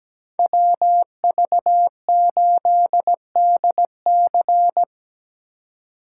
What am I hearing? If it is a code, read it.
Morse code, heard as a keyed tone, about 19 wpm, 700 Hz. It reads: WV8DC
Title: Speed Racing - USA 2x2 Call Signs 17wpm